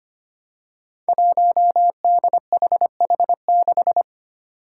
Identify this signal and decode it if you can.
Morse code, heard as a keyed tone, about 25 wpm, 700 Hz. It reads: 1DHH6
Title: Five Challenging Letter-Number Combinations 25wpm